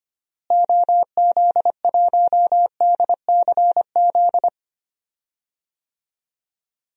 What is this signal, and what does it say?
Morse code, heard as a keyed tone, about 25 wpm, 700 Hz. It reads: OZ1DCZ
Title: DXCC-100 Call Signs - 6 Characters - Repeated 2x - 25wpm